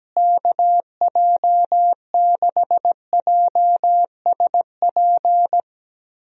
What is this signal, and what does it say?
Morse code, heard as a keyed tone, about 17 wpm, 700 Hz. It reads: KJ6JSP
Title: USA Call Signs - 6 Characters - Repeated 2x - 17wpm